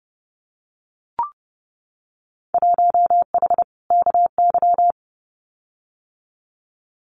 Morse code, heard as a keyed tone, about 31 wpm, 700 Hz. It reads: 1H KY
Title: Winter Field Day Exchanges 30wpm